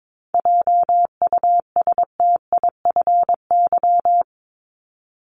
Morse code, heard as a keyed tone, about 22 wpm, 700 Hz. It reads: JUSTIFY
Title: Top 1751-2000 Words 22wpm